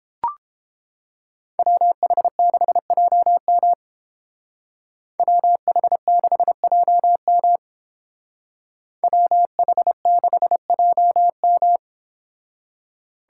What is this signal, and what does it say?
Morse code, heard as a keyed tone, about 30 wpm, 700 Hz. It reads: WH6JM WH6JM WH6JM